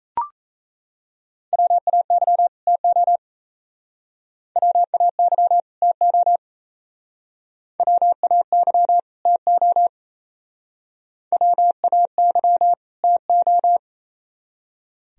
Morse code, heard as a keyed tone, about 37 wpm, 700 Hz. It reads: WAY TO WAY TO WAY TO WAY TO